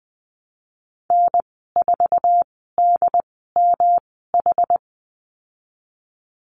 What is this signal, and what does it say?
Morse code, heard as a keyed tone, about 20 wpm, spaced as for 15 wpm, 700 Hz. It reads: N4DMH
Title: 154 Lesson 25g - Letter p - Call Signs 20wpm